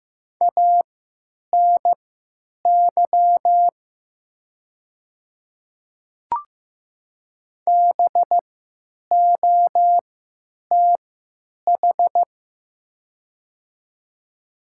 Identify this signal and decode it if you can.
Morse code, heard as a keyed tone, about 15 wpm, spaced as for 9 wpm, 700 Hz. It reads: ANY BOTH
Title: 3x Character Spacing - Top 300 Words - 15wpm